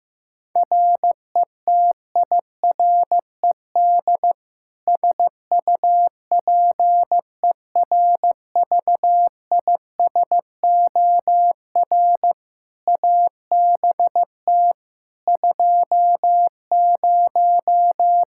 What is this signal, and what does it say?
Morse code, heard as a keyed tone, about 16 wpm, 700 Hz. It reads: RETIRED SUPERVISOR ABT 20
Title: Practice QSO - Exchange 3 - More Info - 15wpm